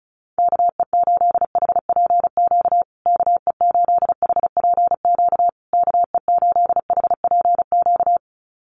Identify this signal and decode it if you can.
Morse code, heard as a keyed tone, about 34 wpm, 700 Hz. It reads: KE8HPQ KE8HPQ KE8HPQ
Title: USA Call Signs Repeated 3x 35wpm